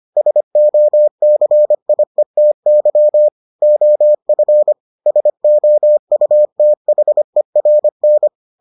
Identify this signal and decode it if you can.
Morse code, heard as a keyed tone, about 25 wpm, 600 Hz. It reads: SOCIETY OF SOUTHERN